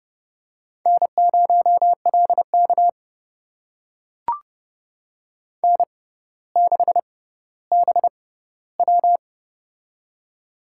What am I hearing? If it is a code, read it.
Morse code, heard as a keyed tone, about 30 wpm, spaced as for 15 wpm, 700 Hz. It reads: N0LK N6BW
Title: USA Call Signs - 4 Characters - 6x Character Spacing - 30wpm